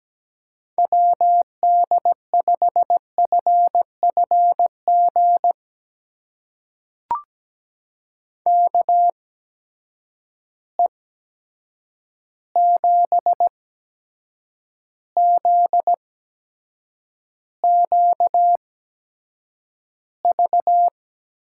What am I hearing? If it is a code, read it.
Morse code, heard as a keyed tone, about 17 wpm, spaced as for 7 wpm, 700 Hz. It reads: WD5FFG KE7ZQV
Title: USA Call Signs - 6 Characters - 8x Character Spacing - 17wpm